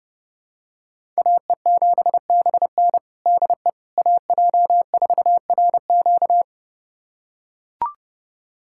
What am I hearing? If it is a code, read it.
Morse code, heard as a keyed tone, about 32 wpm, 700 Hz. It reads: AE7BN DE AJ4RQ
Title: USA Callsign DE Callsign - 5 Characters - 30wpm